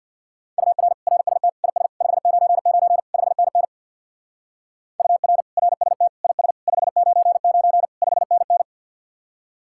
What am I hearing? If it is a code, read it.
Morse code, heard as a keyed tone, about 64 wpm, 690 Hz. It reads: UR RST IS 599 5NN UR RST IS 599 5NN